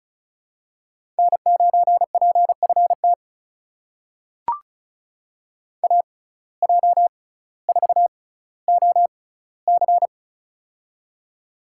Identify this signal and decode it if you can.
Morse code, heard as a keyed tone, about 36 wpm, spaced as for 18 wpm, 700 Hz. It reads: N9PFT AJ4OC
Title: USA Call Signs - 5 Characters - 6x Character Spacing - 35wpm